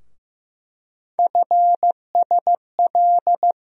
Morse code, heard as a keyed tone, about 15 wpm, 700 Hz. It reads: FSL
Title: USA Call Sign Postfixes - 3 Letters 15wpm